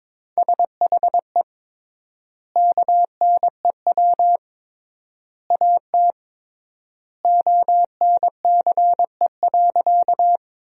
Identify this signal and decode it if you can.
Morse code, heard as a keyed tone, about 23 wpm, 700 Hz. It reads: SHE KNEW AT ONCE.
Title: 3x Word-Spacing - Sentences from Top 400 Words - 22wpm